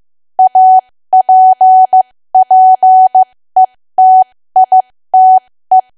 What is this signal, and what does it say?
Morse code, heard as a keyed tone, about 15 wpm, spaced as for 13 wpm, 750 Hz. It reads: APPETITE
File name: cwa / W1AW-Level2 / 13WPM